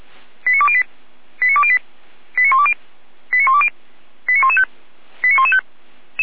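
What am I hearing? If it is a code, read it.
Five-tone selective call (ZVEI), heard as quick runs of five tones, tones of about 70 ms. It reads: 89208 89208 89120 89120 89115 89115
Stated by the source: ZVEI five-tone selective call received by SDR.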